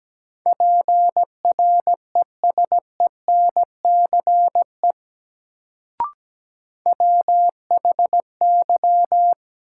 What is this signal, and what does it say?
Morse code, heard as a keyed tone, about 17 wpm, 700 Hz. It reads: PRESENCE WHY
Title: Sets of 3 Words - New Word Drill - 17wpm